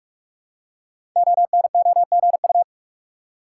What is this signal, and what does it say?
Morse code, heard as a keyed tone, about 44 wpm, 700 Hz. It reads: ONOGU
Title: Code Groups - 5 Letters - 45wpm